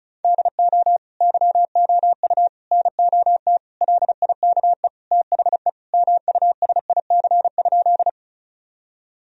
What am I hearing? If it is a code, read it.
Morse code, heard as a keyed tone, about 36 wpm, 700 Hz. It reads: DO YOU NOT LIKE THE MUSIC?
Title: Sentences from Top 1000 Words - Review Part 1 - 35wpm